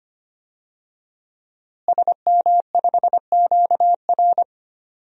Morse code, heard as a keyed tone, about 25 wpm, 700 Hz. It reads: SM5QR
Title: DXCC-100 Call Signs - 5 Characters - Repeated 3x - 25wpm